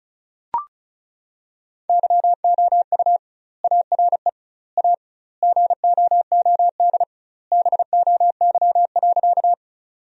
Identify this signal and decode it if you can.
Morse code, heard as a keyed tone, about 36 wpm, 700 Hz. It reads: YOU ARE A GOOD BOY.
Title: 2x Word-Spacing - Sentences from Top 200 Words - 35wpm